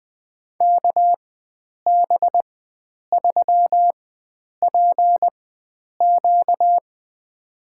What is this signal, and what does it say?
Morse code, heard as a keyed tone, about 21 wpm, spaced as for 9 wpm, 700 Hz. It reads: KB3PQ
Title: USA Call Signs - 5 Characters - 4x Character Spacing - 20wpm